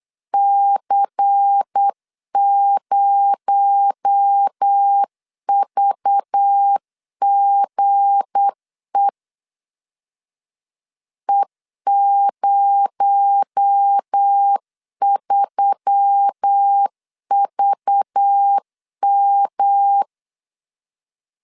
Morse code, heard as a keyed tone, about 8 wpm, 800 Hz. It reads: C0VGE E03VM
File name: LEZIONI / 7 GF9Z / 7 G 8wpm 3118